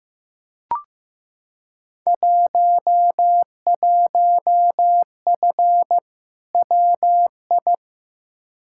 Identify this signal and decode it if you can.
Morse code, heard as a keyed tone, about 15 wpm, 700 Hz. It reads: 11F WI
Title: Field Day Exchanges Encore - 15wpm